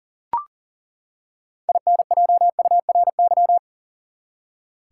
Morse code, heard as a keyed tone, about 39 wpm, 700 Hz. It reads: INJURY
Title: Six Letter Words - 40wpm